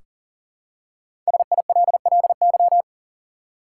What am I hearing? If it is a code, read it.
Morse code, heard as a keyed tone, about 40 wpm, 700 Hz. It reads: SILLY
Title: Five Letter Words - 40wpm